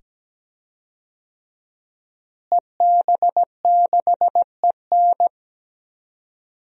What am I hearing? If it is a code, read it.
Morse code, heard as a keyed tone, about 17 wpm, 700 Hz. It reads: EB6EN
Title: DXCC-100 Call Signs - 5 Characters - Repeated 2x - 17wpm